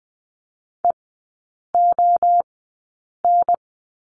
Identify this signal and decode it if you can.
Morse code, heard as a keyed tone, about 20 wpm, spaced as for 8 wpm, 700 Hz. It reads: EON
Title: 035 Lesson 7e - Letter s - Three Characters 20wpm